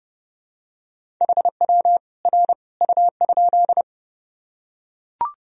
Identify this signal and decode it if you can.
Morse code, heard as a keyed tone, about 30 wpm, 700 Hz. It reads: HW R U?
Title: Sentences from Top 100 Words and Abbreviations 30wpm